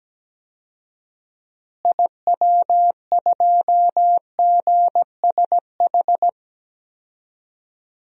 Morse code, heard as a keyed tone, about 17 wpm, 700 Hz. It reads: IW2GSH
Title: DXCC-100 Call Signs - 6 Characters - Repeated 3x - 17wpm